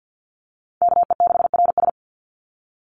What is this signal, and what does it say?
Morse code, heard as a keyed tone, about 49 wpm, 700 Hz. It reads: KE6RS
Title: NA SOTA Chaser Callsigns 50wpm